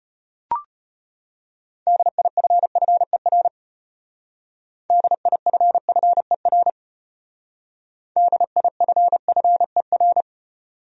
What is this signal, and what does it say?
Morse code, heard as a keyed tone, about 33 wpm, 700 Hz. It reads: DIFFER DIFFER DIFFER